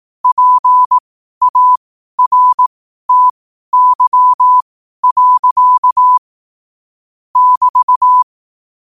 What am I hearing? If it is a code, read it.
Morse code, heard as a keyed tone, about 18 wpm, spaced as for 12 wpm, 1000 Hz. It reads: PARTY. <BT>
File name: qotdfiles / 12wpm